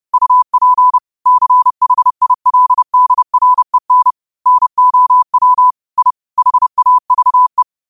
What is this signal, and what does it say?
Morse code, heard as a keyed tone, about 30 wpm, 1000 Hz. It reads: UP CHILDREN NOW I HAVE